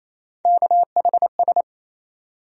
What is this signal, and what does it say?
Morse code, heard as a keyed tone, about 28 wpm, 700 Hz. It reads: KHS
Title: USA Call Sign Postfixes - 3 Letters 28wpm